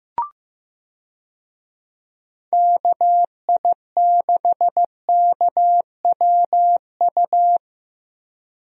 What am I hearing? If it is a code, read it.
Morse code, heard as a keyed tone, about 15 wpm, 700 Hz. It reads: KI6KWU
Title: USA Call Signs - 6 Characters - Repeated 2x - 15wpm